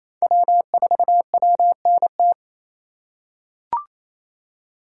Morse code, heard as a keyed tone, about 28 wpm, 700 Hz. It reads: W4WNT
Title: K1USN SST QSO Practice - 28wpm